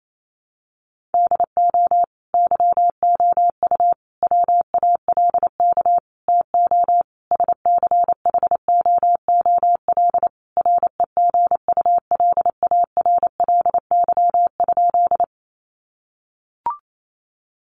Morse code, heard as a keyed tone, about 28 wpm, 700 Hz. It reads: DO YOU WALK TO SCHOOL REGULARLY?
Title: Sentences from Top 1400 Words 28wpm